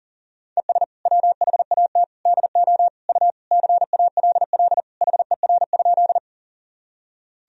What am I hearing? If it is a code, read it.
Morse code, heard as a keyed tone, about 40 wpm, 700 Hz. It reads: ES WHAT DO U CALL HER?
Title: Sentences from Top 100 Words and Abbreviations 40wpm